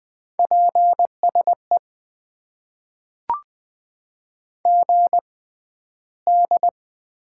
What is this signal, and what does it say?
Morse code, heard as a keyed tone, about 20 wpm, spaced as for 12 wpm, 700 Hz. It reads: PSE GD
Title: Top 90 QSO Elements - 6x Character Spacing - 20wpm